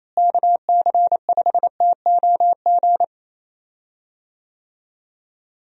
Morse code, heard as a keyed tone, about 28 wpm, 700 Hz. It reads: KC5TOG
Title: USA Call Signs - 6 Characters - Repeated 2x - 28wpm